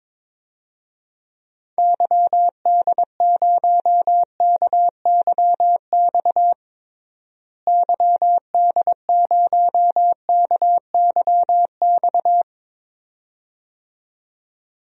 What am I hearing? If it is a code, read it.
Morse code, heard as a keyed tone, about 22 wpm, 700 Hz. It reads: YD0KYX YD0KYX
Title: DXCC-100 Call Signs - 6 Characters - Repeated 2x - 22wpm